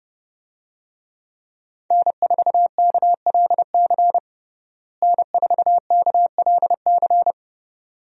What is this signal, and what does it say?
Morse code, heard as a keyed tone, about 30 wpm, 700 Hz. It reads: N4KLC N4KLC
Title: USA Call Signs - 5 Characters - Repeated 2x - 30wpm